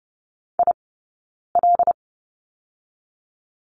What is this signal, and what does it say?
Morse code, heard as a keyed tone, about 30 wpm, 700 Hz. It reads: I L
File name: Lesson 20d - Number 3 - Two Characters 30wpm